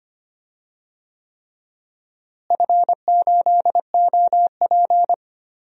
Morse code, heard as a keyed tone, about 25 wpm, 700 Hz. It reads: F8OP
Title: DXCC-100 Call Signs - 4 Characters - Repeated 3x - 25wpm